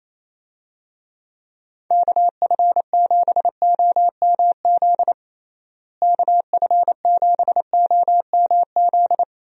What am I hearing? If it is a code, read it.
Morse code, heard as a keyed tone, about 28 wpm, 700 Hz. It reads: KF7OMZ KF7OMZ
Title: USA Call Signs - 6 Characters - Repeated 2x - 28wpm